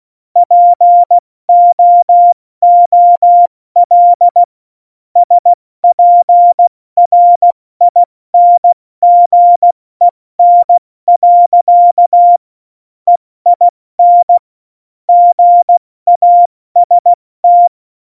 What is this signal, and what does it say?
Morse code, heard as a keyed tone, about 16 wpm, 700 Hz. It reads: POOL SPRINGEN. EIN GAST
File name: Morsekurs / Witze / Witz 17. November 2020, 70 F80